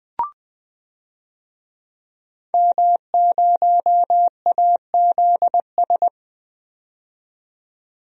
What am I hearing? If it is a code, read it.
Morse code, heard as a keyed tone, about 20 wpm, 700 Hz. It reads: M0AZS
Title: DXCC-100 Call Signs - 5 Characters - 20wpm